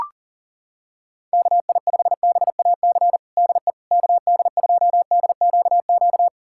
Morse code, heard as a keyed tone, about 40 wpm, 700 Hz. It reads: KI5BAC DE KD2DQQ
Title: USA Callsign DE Callsign - 6 Characters - 40wpm